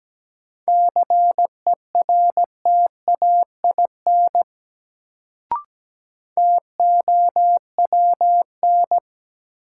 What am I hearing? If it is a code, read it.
Morse code, heard as a keyed tone, about 17 wpm, 700 Hz. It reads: CERTAIN TOWN
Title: Sentences from Top 500 Words - New Word Drill - 17wpm